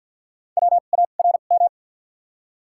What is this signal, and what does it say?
Morse code, heard as a keyed tone, about 53 wpm, 700 Hz. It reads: W A R M
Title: 2x Character Spacing - Top 500 Words - 50wpm